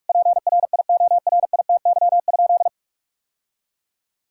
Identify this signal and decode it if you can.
Morse code, heard as a keyed tone, about 43 wpm, 700 Hz. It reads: PRIORITY?
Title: Sentences from Top 1600 Words 45wpm